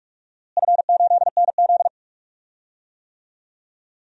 Morse code, heard as a keyed tone, about 46 wpm, 700 Hz. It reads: F8NZ
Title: DXCC-100 Call Signs - 4 Characters - Repeated 2x - 45wpm